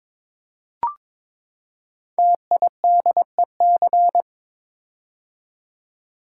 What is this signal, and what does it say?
Morse code, heard as a keyed tone, about 22 wpm, 700 Hz. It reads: TIDEC
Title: Code Groups - 5 Letters - 22wpm